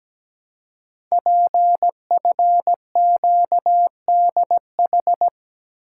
Morse code, heard as a keyed tone, about 17 wpm, 700 Hz. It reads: PFQDH
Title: Code Groups - 5 Letters - 17wpm